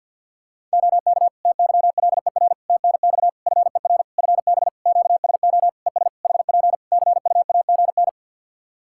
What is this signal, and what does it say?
Morse code, heard as a keyed tone, about 48 wpm, 700 Hz. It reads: OK TYLER TNX FER FB QSO ES HP CUAGN
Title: Practice QSO - Exchange 4 - Ending - 50wpm